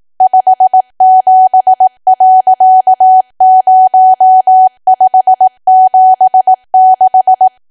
Morse code, heard as a keyed tone, about 18 wpm, 750 Hz. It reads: 57.0576
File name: cwa / W1AW-Level2 / 18WPM